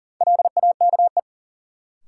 Morse code, heard as a keyed tone, about 40 wpm, 700 Hz. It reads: LAKE
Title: Sentences from Top 1000 Words - Review Part 3 - New Word Drill - 40wpm